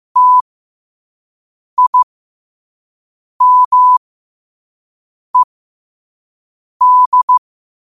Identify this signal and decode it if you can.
Morse code, heard as a keyed tone, about 15 wpm, spaced as for 5 wpm, 1000 Hz. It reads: TIMED
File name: qotdfiles / 5wpm